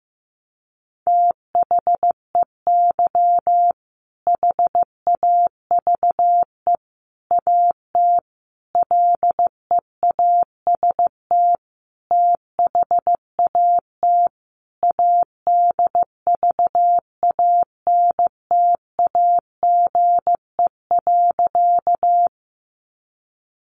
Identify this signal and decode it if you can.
Morse code, heard as a keyed tone, about 15 wpm, 700 Hz. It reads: THEY HAVE AT LEAST THAT ADVANTAGE.
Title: Sentences from Top 900 Words 15wpm